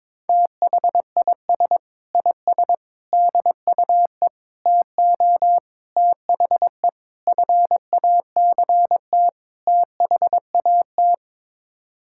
Morse code, heard as a keyed tone, about 22 wpm, 700 Hz. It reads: THIS IS DUE TO THE FACT THAT
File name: practice / 22wpm / Sets of 7 Words - 22wpm